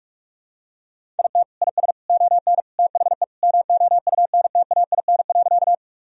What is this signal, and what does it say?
Morse code, heard as a keyed tone, about 46 wpm, 700 Hz. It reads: IT IS ON THE MOUNTAIN.
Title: Sentences from Top 1000 Words - Review Part 1 - 45wpm